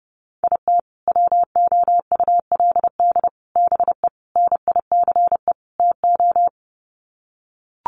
Morse code, heard as a keyed tone, about 30 wpm, 700 Hz. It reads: IT WOULD BE NICE TO
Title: Sets of 5 Words Encore2 30wpm